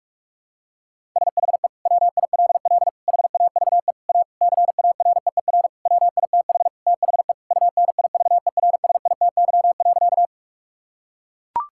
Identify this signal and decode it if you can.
Morse code, heard as a keyed tone, about 41 wpm, 700 Hz. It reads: SHE WILL HAVE A CAREER WITH THE UNIVERSITY.
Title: Sentences from Top 1000 Words - Review Part 6 - 45wpm